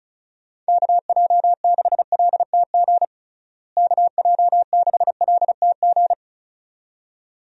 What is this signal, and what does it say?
Morse code, heard as a keyed tone, about 36 wpm, 700 Hz. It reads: KJ6LTG KJ6LTG
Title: USA Call Signs - 6 Characters - Repeated 2x - 35wpm